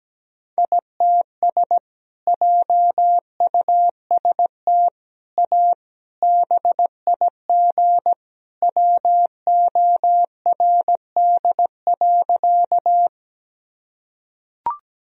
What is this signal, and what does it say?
Morse code, heard as a keyed tone, about 17 wpm, 700 Hz. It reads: ITS JUST A BIG WORD.
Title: Sentences from Top 1000 Words - Review Part 1 - 17wpm